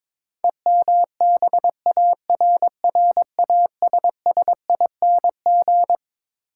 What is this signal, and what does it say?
Morse code, heard as a keyed tone, about 22 wpm, 700 Hz. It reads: EMBARRASSING
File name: Twelve Letter Words - 22wpm